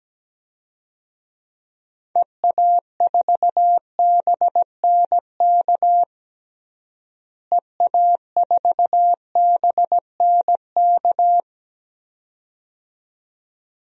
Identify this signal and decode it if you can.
Morse code, heard as a keyed tone, about 17 wpm, 700 Hz. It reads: EA4BNK EA4BNK
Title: DXCC-100 Call Signs - 6 Characters - Repeated 2x - 17wpm